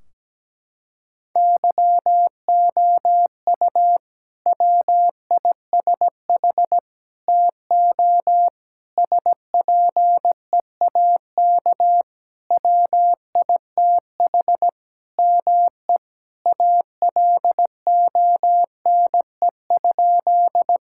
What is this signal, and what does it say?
Morse code, heard as a keyed tone, about 17 wpm, 700 Hz. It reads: YOU WISH TO SPEAK WITH ME ALONE?
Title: Sentences from Top 1600 Words 17wpm